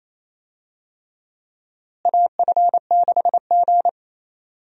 Morse code, heard as a keyed tone, about 28 wpm, 700 Hz. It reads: AF6G
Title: USA Call Signs - 4 Characters - Repeated 3x - 28wpm